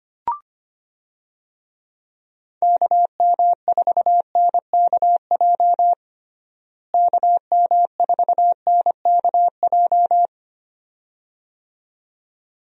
Morse code, heard as a keyed tone, about 25 wpm, 700 Hz. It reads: KM4NKJ KM4NKJ
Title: USA Call Signs - 6 Characters - Repeated 2x - 25wpm